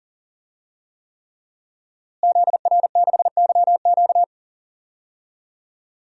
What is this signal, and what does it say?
Morse code, heard as a keyed tone, about 39 wpm, 700 Hz. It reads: ZR6YQ
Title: DXCC-100 Call Signs - 5 Characters - Repeated 3x - 40wpm